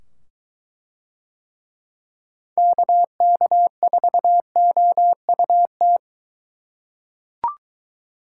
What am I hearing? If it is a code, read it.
Morse code, heard as a keyed tone, about 21 wpm, 700 Hz. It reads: KK4OUT